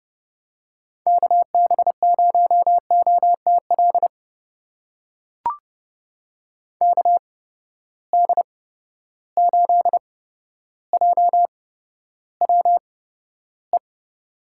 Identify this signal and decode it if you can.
Morse code, heard as a keyed tone, about 30 wpm, spaced as for 13 wpm, 700 Hz. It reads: KB0OTL KD8JWE
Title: USA Call Signs - 6 Characters - 8x Character Spacing - 30wpm